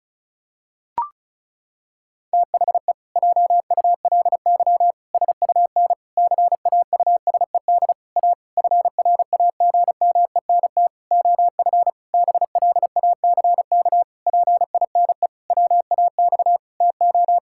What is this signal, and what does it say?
Morse code, heard as a keyed tone, about 34 wpm, 700 Hz. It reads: THE JULY SUN CAUSED A FRAGMENT OF BLACK PINE WAX TO
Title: Pangram Challenge v2 - 35wpm